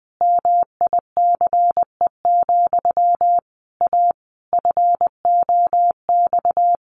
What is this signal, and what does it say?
Morse code, heard as a keyed tone, about 20 wpm, 700 Hz. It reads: MICE, A FOX
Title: A Sound Of Thunder 20wpm